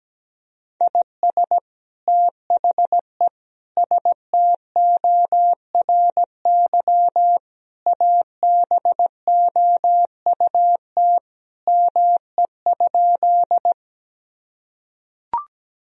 Morse code, heard as a keyed tone, about 17 wpm, 700 Hz. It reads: IS THE STORY ABOUT ME?
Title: Sentences from Top 1000 Words - Review Part 1 - 17wpm